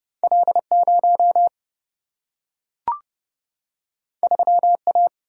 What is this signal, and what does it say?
Morse code, heard as a keyed tone, about 30 wpm, 700 Hz. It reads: L0 3A
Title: Single-Letter Single-Number 30wpm